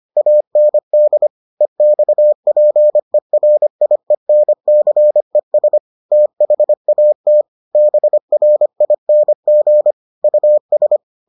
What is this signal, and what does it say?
Morse code, heard as a keyed tone, about 25 wpm, 600 Hz. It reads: AND EXPERIENCES THAT BRING US